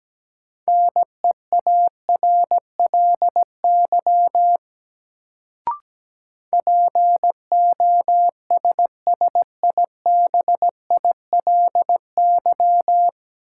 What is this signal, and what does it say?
Morse code, heard as a keyed tone, about 17 wpm, 700 Hz. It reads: NEARLY POSSIBILY
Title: Sentences from Top 1300 Words - New Word Drill - 17wpm